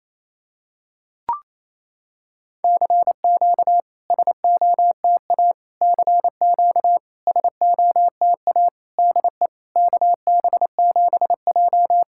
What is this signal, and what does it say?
Morse code, heard as a keyed tone, about 28 wpm, 700 Hz. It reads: CQ SOTA CQ SOTA DE KB7J
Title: SOTA QSO Practice 28wpm